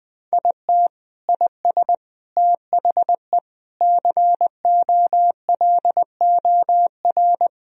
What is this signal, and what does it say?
Morse code, heard as a keyed tone, about 20 wpm, 700 Hz. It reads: IT IS THE COLOR
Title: Sentences from Top 1000 Words - Review Part 3 - 20wpm